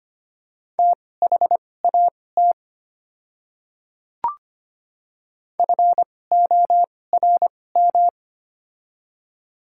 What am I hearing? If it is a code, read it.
Morse code, heard as a keyed tone, about 25 wpm, spaced as for 18 wpm, 700 Hz. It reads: THAT FORM
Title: 2x Character Spacing - Top 200 Words - 25wpm